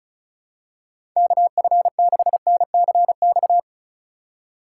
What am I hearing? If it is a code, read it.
Morse code, heard as a keyed tone, about 34 wpm, 700 Hz. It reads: KF6NCX
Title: K1USN SST QSO Practice - 35wpm